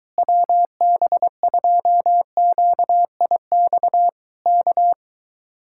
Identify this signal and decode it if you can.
Morse code, heard as a keyed tone, about 24 wpm, 700 Hz. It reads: WB2QIX K